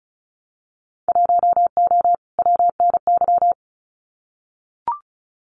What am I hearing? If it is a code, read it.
Morse code, heard as a keyed tone, about 34 wpm, 700 Hz. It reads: 1O WNY
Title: Winter Field Day Exchanges 35wpm